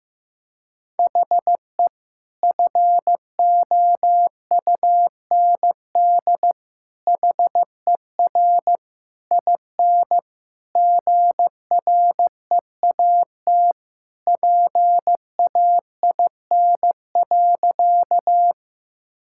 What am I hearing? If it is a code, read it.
Morse code, heard as a keyed tone, about 15 wpm, 700 Hz. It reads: HE FOUND HER IN GREAT PAIN.
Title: Sentences from Top 1600 Words 15wpm